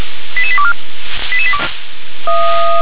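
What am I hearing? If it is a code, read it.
Five-tone selective call (ZVEI), heard as quick runs of five tones, tones of about 70 ms. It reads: 88824
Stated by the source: ZVEI five-tone selective call received by SDR.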